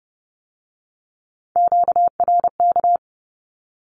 Morse code, heard as a keyed tone, about 30 wpm, 700 Hz. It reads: QRK
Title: Q-Codes with Spoken Meaning - 30wpm